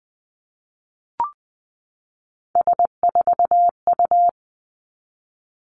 Morse code, heard as a keyed tone, about 20 wpm, 700 Hz. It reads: S4U
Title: Three Challenging Letter-Number Combinations 20wpm